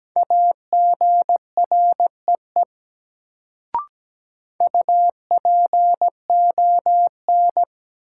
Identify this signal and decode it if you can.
Morse code, heard as a keyed tone, about 17 wpm, 700 Hz. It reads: AGREE UPON
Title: Sentences from Top 1000 Words - Review Part 6 - New Word Drill - 17wpm